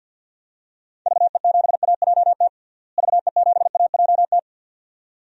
Morse code, heard as a keyed tone, about 51 wpm, 700 Hz. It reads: VE7AJT VE7AJT
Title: DXCC-100 Call Signs - 6 Characters - Repeated 2x - 50wpm